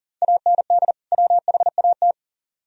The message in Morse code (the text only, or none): AND WHAT